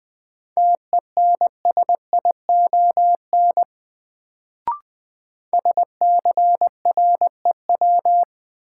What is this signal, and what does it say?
Morse code, heard as a keyed tone, about 21 wpm, 700 Hz. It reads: TENSION SCREW
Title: Sentences from Top 1600 Words - New Word Drill - 20wpm